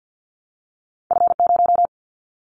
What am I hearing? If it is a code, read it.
Morse code, heard as a keyed tone, about 50 wpm, 700 Hz. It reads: F0
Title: Callsign Prefixes for Top 25 DXCC - 50wpm